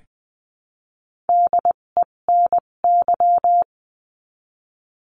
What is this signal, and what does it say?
Morse code, heard as a keyed tone, about 20 wpm, spaced as for 17 wpm, 700 Hz. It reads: DENY
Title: 257 Lesson 42b - Words 14-20wpm